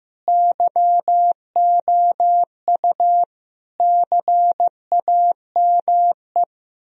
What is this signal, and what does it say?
Morse code, heard as a keyed tone, about 15 wpm, 700 Hz. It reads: YOU CAME
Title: Sentences from Top 1000 Words - Review Part 3 - 15wpm